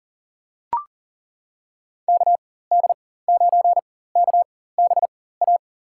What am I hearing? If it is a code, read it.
Morse code, heard as a keyed tone, about 40 wpm, spaced as for 19 wpm, 700 Hz. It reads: KD9KBA
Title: USA Call Signs - 6 Characters - 4x Character Spacing - 40wpm